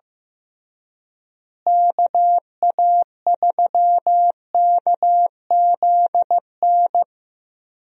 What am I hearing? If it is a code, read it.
Morse code, heard as a keyed tone, about 15 wpm, 700 Hz. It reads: KA3KZN
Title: USA Call Signs - 6 Characters - Repeated 3x - 15wpm